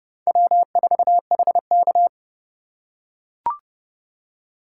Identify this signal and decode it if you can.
Morse code, heard as a keyed tone, about 30 wpm, 700 Hz. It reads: W4HK
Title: USA Call Signs - 3x Character Spacing - 30wpm